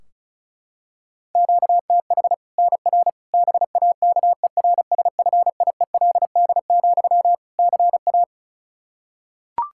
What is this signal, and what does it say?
Morse code, heard as a keyed tone, about 36 wpm, 700 Hz. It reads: QTH NR BAKERSFIELD, CA
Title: K1USN SST QSO Practice 35wpm